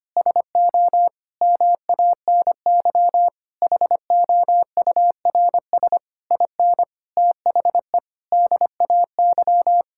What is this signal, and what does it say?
Morse code, heard as a keyed tone, about 25 wpm, 700 Hz. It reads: SO MANY HOURS IN THE DAY
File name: practice / 25wpm / Sentences with 9-Word Phrases - 25wpm